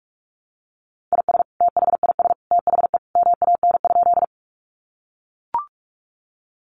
Morse code, heard as a keyed tone, about 44 wpm, 700 Hz. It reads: IS THIS THE MAN?
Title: Sentences from Top 200 Words 45wpm